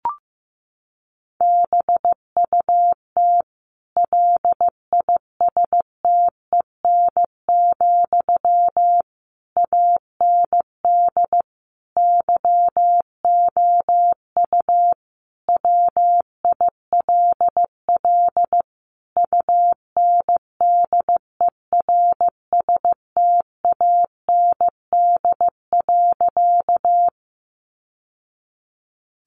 Morse code, heard as a keyed tone, about 15 wpm, 700 Hz. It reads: BUT LISTEN, AND YOU WILL UNDERSTAND.
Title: Sentences from Top 600 Words 15wpm